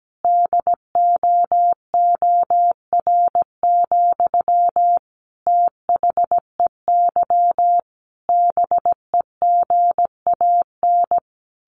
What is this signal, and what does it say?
Morse code, heard as a keyed tone, about 17 wpm, 700 Hz. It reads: DOOR, THEY BEGAN
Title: Sentences from Top 1400 Words 17wpm